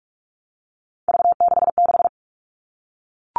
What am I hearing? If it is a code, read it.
Morse code, heard as a keyed tone, about 44 wpm, 700 Hz. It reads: V66
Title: All Cut Numbers - 3-Digit 45wpm